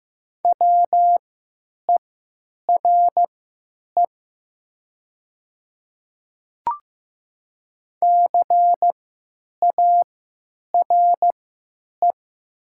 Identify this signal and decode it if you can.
Morse code, heard as a keyed tone, about 15 wpm, spaced as for 9 wpm, 700 Hz. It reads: WERE CARE
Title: 3x Character Spacing - Top 300 Words - 15wpm